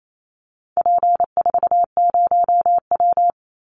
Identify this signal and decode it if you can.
Morse code, heard as a keyed tone, about 28 wpm, 700 Hz. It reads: P40W
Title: Call Signs Repeated 3x - 28wpm